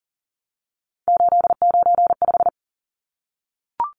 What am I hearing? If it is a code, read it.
Morse code, heard as a keyed tone, about 40 wpm, 700 Hz. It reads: 895
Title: Numbers - 3-Digit 40wpm